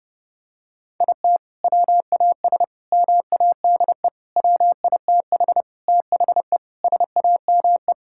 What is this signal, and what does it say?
Morse code, heard as a keyed tone, about 32 wpm, 700 Hz. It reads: IT WAS MADE WITH THE SAME
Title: Sentences from Top 1000 Words - Review Part 5 - 30wpm